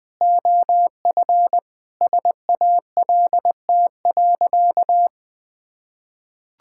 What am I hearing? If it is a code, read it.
Morse code, heard as a keyed tone, about 20 wpm, 700 Hz. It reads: OF SALT.
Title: Sentences with 7-Word Phrases - 20wpm